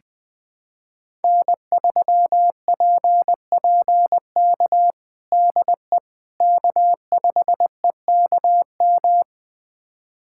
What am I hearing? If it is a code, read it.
Morse code, heard as a keyed tone, about 21 wpm, 700 Hz. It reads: N3PPK DE K5EKM
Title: USA Callsign DE Callsign - 5 Characters - 20wpm